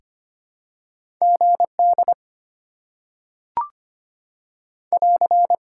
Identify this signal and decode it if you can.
Morse code, heard as a keyed tone, about 25 wpm, 690 Hz. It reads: GD <AR>
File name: Top 90 QSO Elements - 6x Character Spacing - 25wpm